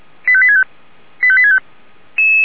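Five-tone selective call (ZVEI), heard as quick runs of five tones, tones of about 70 ms. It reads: 85865 85865
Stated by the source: Five-tone ZVEI selective call 85863, received on SDR.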